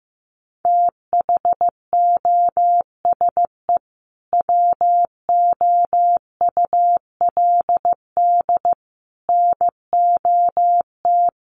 Morse code, heard as a keyed tone, about 15 wpm, 700 Hz. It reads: THOSE WOULD NOT
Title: Sentences from Top 1300 Words 15wpm